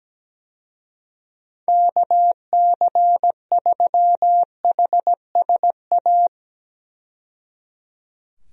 Morse code, heard as a keyed tone, about 17 wpm, 700 Hz. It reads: KC3HSA
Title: USA Call Signs - 6 Characters - 17wpm